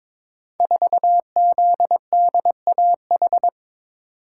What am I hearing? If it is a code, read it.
Morse code, heard as a keyed tone, about 22 wpm, 690 Hz. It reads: 4ZDAH
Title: Code Groups - 5 Letter:Numbers - 22wpm